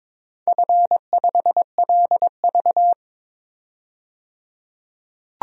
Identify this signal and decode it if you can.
Morse code, heard as a keyed tone, about 22 wpm, 700 Hz. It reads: F5LV
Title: DXCC-100 Call Signs - 4 Characters - 22wpm